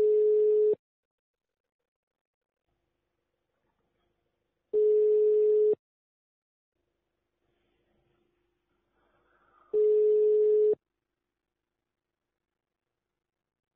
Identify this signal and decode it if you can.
Ringback tone (Europe, 425 Hz).